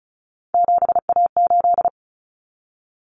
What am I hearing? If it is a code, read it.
Morse code, heard as a keyed tone, about 34 wpm, 700 Hz. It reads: 7A8
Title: Common Cut Numbers - 3-Digit 35wpm